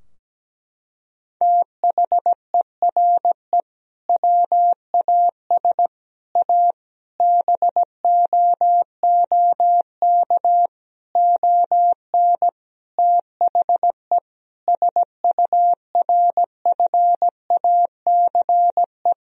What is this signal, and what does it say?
Morse code, heard as a keyed tone, about 17 wpm, 700 Hz. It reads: THERE WAS A BOOK ON THE SURFACE
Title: Sentences from Top 1000 Words - Review Part 3 - 17wpm